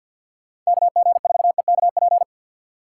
Morse code, heard as a keyed tone, about 64 wpm, 700 Hz. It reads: KG4EKP